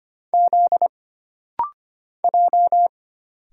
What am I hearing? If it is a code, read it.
Morse code, heard as a keyed tone, about 25 wpm, 700 Hz. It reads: Z J
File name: Single Character (Letter:Number:Punctuation:Prosign) - Rapid-Fire - 25wpm